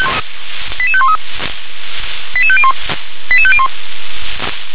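Five-tone selective call (ZVEI), heard as quick runs of five tones, tones of about 70 ms. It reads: 88513 88501 88501
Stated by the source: Five-tone ZVEI selective call received on an SDR.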